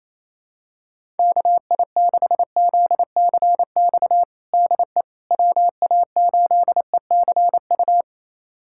Morse code, heard as a keyed tone, about 27 wpm, 700 Hz. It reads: KI6ZCX DE WA8ECU
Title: USA Callsign DE Callsign - 6 Characters - 28wpm